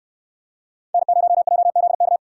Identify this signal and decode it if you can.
Morse code, heard as a keyed tone, about 66 wpm, 700 Hz. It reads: N0JZK